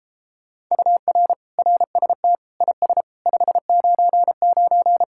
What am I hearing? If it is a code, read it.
Morse code, heard as a keyed tone, about 33 wpm, 700 Hz. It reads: UR RST IS 599